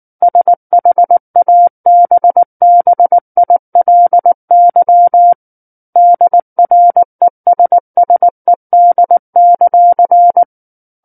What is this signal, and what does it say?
Morse code, heard as a keyed tone, about 19 wpm, 700 Hz. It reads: SHABBILY DRESSED;
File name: Book One / 10-Chapter 10 - 19 WPM